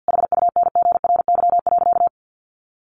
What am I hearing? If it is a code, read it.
Morse code, heard as a keyed tone, about 50 wpm, 700 Hz. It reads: HUNGRY.
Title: Sentences from Top 1400 Words 50wpm